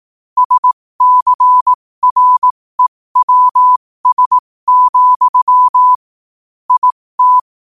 Morse code, heard as a keyed tone, about 18 wpm, spaced as for 15 wpm, 1000 Hz. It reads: SCREWS, IT
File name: qotdfiles / 15wpm